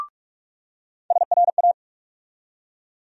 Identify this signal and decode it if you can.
Morse code, heard as a keyed tone, about 41 wpm, 700 Hz. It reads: SRA
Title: USA Call Sign Postfixes - 3 Letters 45wpm